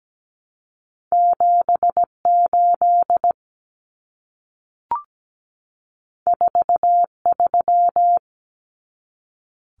Morse code, heard as a keyed tone, about 17 wpm, 700 Hz. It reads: 78 43
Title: Common Cut Numbers - 2-Digit 17wpm